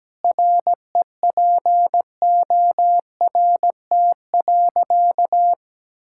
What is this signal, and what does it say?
Morse code, heard as a keyed tone, about 17 wpm, 700 Hz. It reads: REPORT.
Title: Sentences from Top 1000 Words - Review Part 6 - 17wpm